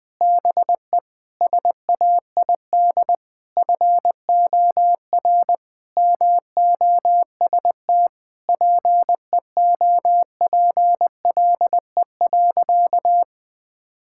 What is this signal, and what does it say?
Morse code, heard as a keyed tone, about 20 wpm, 700 Hz. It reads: BE SAID FOR MOST PEOPLE.
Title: Sentences with 6-Word Phrases - Spoken Beginning - 20wpm